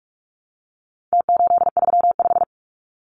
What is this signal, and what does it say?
Morse code, heard as a keyed tone, about 44 wpm, 700 Hz. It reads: T835
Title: Numbers - Contest Style - 4-Digit 45wpm